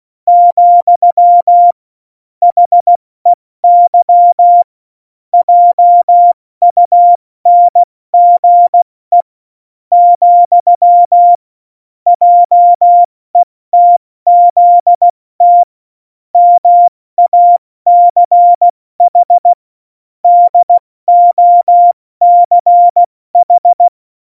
Morse code, heard as a keyed tone, about 16 wpm, 700 Hz. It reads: , HEY JUNGE , JETZT MACH DOCH
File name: Morsekurs / Witze / Witz 04. Dezember, 70 F80